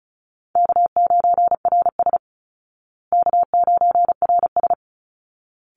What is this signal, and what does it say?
Morse code, heard as a keyed tone, about 34 wpm, 700 Hz. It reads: K9RS K9RS
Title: Easy Call Signs Repeated 2x - 35wpm